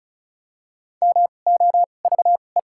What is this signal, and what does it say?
Morse code, heard as a keyed tone, about 36 wpm, spaced as for 25 wpm, 700 Hz. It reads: MOVE
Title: 2x Character Spacing - Top 500 Words - 35wpm